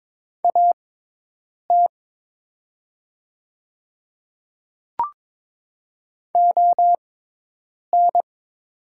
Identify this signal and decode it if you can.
Morse code, heard as a keyed tone, about 22 wpm, spaced as for 8 wpm, 700 Hz. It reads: AT ON
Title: 6x Character Spacing - Top 100 Words - 22wpm